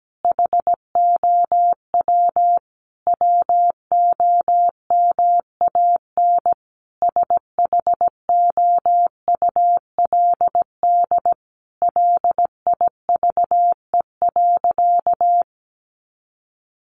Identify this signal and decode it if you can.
Morse code, heard as a keyed tone, about 17 wpm, 700 Hz. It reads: HOW WOMAN SHOULD LIVE.
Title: Sentences from Top 1100 Words 17wpm